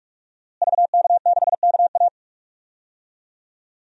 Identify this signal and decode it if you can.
Morse code, heard as a keyed tone, about 43 wpm, 700 Hz. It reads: VK6KA
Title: DXCC-100 Call Signs - 5 Characters - Repeated 2x - 45wpm